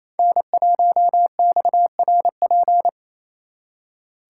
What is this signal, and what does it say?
Morse code, heard as a keyed tone, about 30 wpm, 700 Hz. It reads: N1XRP
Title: Speed Racing - USA 1x3 Call Signs 28wpm